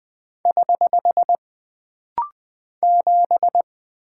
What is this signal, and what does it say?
Morse code, heard as a keyed tone, about 20 wpm, 700 Hz. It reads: <HH> 7
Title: Single Character (Letter:Number:Punctuation:Prosign) - Rapid-Fire - 20wpm